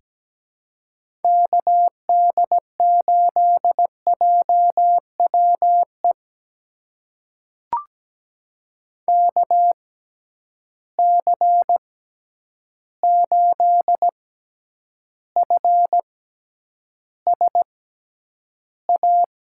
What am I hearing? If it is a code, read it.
Morse code, heard as a keyed tone, about 17 wpm, spaced as for 9 wpm, 700 Hz. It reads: KD8JWE KC8FSA
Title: USA Call Signs - 6 Characters - 6x Character Spacing - 17wpm